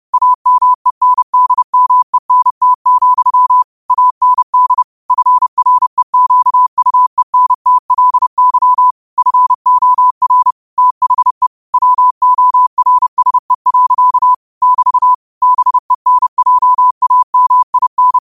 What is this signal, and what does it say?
Morse code, heard as a keyed tone, about 30 wpm, 1000 Hz. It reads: AMENDMENT, AND FREQUENTLY FOR THE WORSE. <BT> BENJAMIN